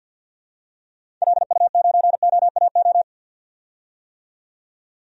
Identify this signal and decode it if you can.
Morse code, heard as a keyed tone, about 46 wpm, 700 Hz. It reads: LU9OAO
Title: DXCC-100 Call Signs - 6 Characters - 50wpm